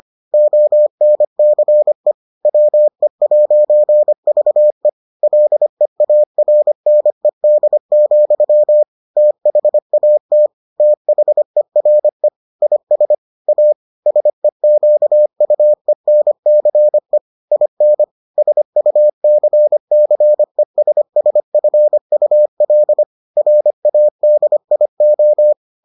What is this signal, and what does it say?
Morse code, heard as a keyed tone, about 25 wpm, 600 Hz. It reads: ONCE WE'VE LEARNED, THAT THERE IS A SEQUENCE IN SUCCESSFUL RADIO